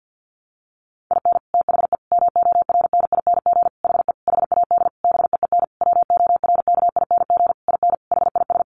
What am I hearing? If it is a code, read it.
Morse code, heard as a keyed tone, about 50 wpm, 700 Hz. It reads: IN THE MORNING HE HAD BEEN WORKING IN HIS
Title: Sentences from Top 1200 Words 50wpm